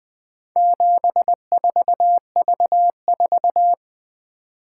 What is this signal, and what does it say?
Morse code, heard as a keyed tone, about 20 wpm, 700 Hz. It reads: 74V4
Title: Four Letter-Number Combinations - 20wpm